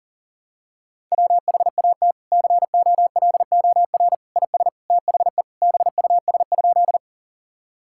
Morse code, heard as a keyed tone, about 37 wpm, 700 Hz. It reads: WHAT COLOR IS THE BUS?
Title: Sentences from Top 1000 Words - Review Part 6 - 40wpm